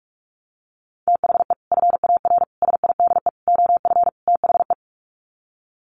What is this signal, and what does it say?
Morse code, heard as a keyed tone, about 44 wpm, 700 Hz. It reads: THE FAR SIDE OF THE
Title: Sets of 5 Words 45wpm